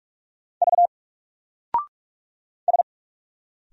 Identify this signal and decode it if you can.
Morse code, heard as a keyed tone, about 44 wpm, 700 Hz. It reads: V S
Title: Single Character (Letter:Number:Punctuation:Prosign) 45wpm